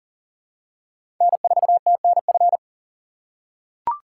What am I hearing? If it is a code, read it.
Morse code, heard as a keyed tone, about 40 wpm, 700 Hz. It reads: N4TNF
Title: USA Call Signs - 5 Characters - 3x Character Spacing - 40wpm